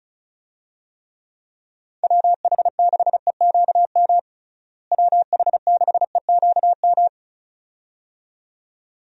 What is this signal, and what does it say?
Morse code, heard as a keyed tone, about 36 wpm, 700 Hz. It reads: WH6EQM WH6EQM
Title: USA Call Signs - 6 Characters - Repeated 2x - 35wpm